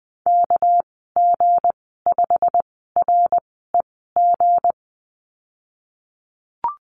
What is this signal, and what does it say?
Morse code, heard as a keyed tone, about 20 wpm, spaced as for 14 wpm, 700 Hz. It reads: KG5REG
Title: 200 Lesson 32g - Letter k - Call Signs 20wpm